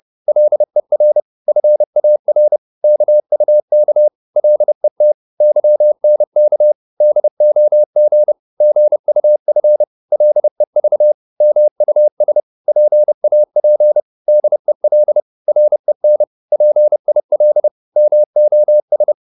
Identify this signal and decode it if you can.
Morse code, heard as a keyed tone, about 30 wpm, 600 Hz. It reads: LER FAR KUK LET YNK DOG GUF LEV MUS PAP DEL REN PIL MOS
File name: Korte ord DK 30wpm